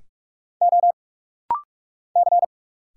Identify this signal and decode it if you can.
Morse code, heard as a keyed tone, about 45 wpm, 700 Hz. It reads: O C
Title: Single Letter-Number Alt - Rapid-Fire 45wpm